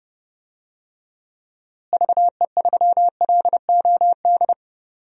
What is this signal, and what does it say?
Morse code, heard as a keyed tone, about 29 wpm, 700 Hz. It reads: VE3LOD
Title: DXCC-100 Call Signs - 6 Characters - Repeated 2x - 30wpm